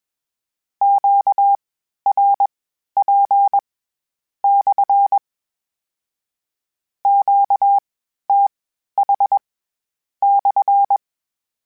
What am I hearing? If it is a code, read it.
Morse code, heard as a keyed tone, about 21 wpm, spaced as for 11 wpm, 800 Hz. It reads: QRP/ QTH/
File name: Q Calls / Q Calls-mp3s / Q Calls SPSL1 @20